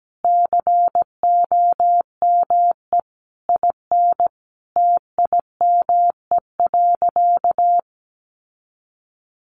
Morse code, heard as a keyed tone, about 17 wpm, 700 Hz. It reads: COME IN TIME.
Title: Sentences from Top 600 Words 17wpm